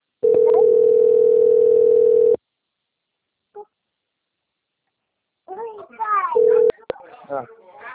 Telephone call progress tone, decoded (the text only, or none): ringback tone